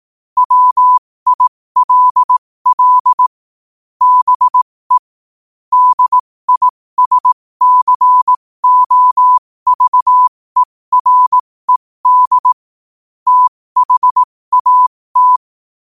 Morse code, heard as a keyed tone, about 18 wpm, spaced as for 15 wpm, 1000 Hz. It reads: WILL BE DISCOVERED THAT
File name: qotdfiles / 15wpm